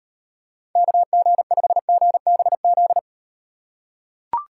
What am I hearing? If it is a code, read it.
Morse code, heard as a keyed tone, about 35 wpm, 700 Hz. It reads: KG5GBZ